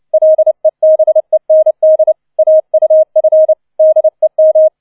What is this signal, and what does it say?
Morse code, heard as a keyed tone, about 29 wpm, 630 Hz. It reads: LEBEND AUF DEM